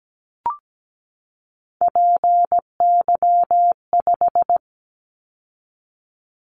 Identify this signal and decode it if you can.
Morse code, heard as a keyed tone, about 17 wpm, 700 Hz. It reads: PY5
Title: Callsign Prefixes for Top 50 DXCC - 17wpm